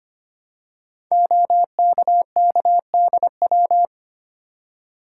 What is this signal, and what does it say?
Morse code, heard as a keyed tone, about 25 wpm, 700 Hz. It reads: OKKDW
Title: Code Groups - 5 Letter:Numbers - 25wpm